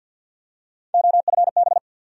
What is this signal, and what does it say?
Morse code, heard as a keyed tone, about 46 wpm, 700 Hz. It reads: OFB
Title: USA Call Sign Postfixes - 3 Letters 50wpm